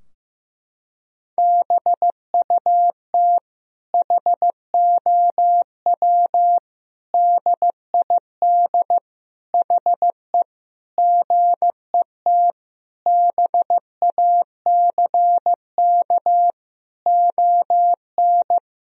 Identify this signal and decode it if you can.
Morse code, heard as a keyed tone, about 15 wpm, 700 Hz. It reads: BUT HOW DID HE GET BACK ON
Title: Sentences from Top 1000 Words - Review Part 4 - 15wpm